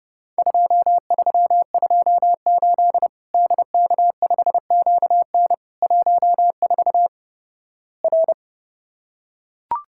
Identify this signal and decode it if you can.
Morse code, heard as a keyed tone, about 30 wpm, 700 Hz. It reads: 2328 DK5QN 14 R
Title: Worked All Europe DX Contest - QTC Exchange - 30wpm